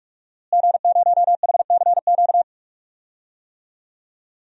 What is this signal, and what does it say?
Morse code, heard as a keyed tone, about 45 wpm, 700 Hz. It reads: G0HCQ
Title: DXCC-100 Call Signs - 5 Characters - Repeated 3x - 45wpm